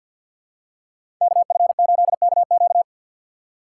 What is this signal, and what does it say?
Morse code, heard as a keyed tone, about 52 wpm, 700 Hz. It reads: KF8KQ
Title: USA Call Signs - 2x Character Spacing - 50wpm